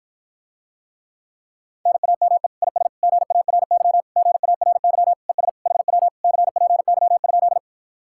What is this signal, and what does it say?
Morse code, heard as a keyed tone, about 52 wpm, 700 Hz. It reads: NAME IS GARY GARY ES HW CPY?